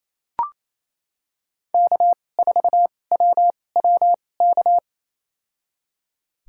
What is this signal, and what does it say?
Morse code, heard as a keyed tone, about 28 wpm, spaced as for 20 wpm, 700 Hz. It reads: K4WWK
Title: USA Call Signs - 5 Characters - 2x Character Spacing - 28wpm